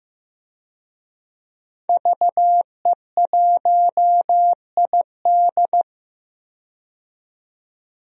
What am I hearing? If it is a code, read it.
Morse code, heard as a keyed tone, about 15 wpm, 700 Hz. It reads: VE1ID
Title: DXCC-100 Call Signs - 5 Characters - 15wpm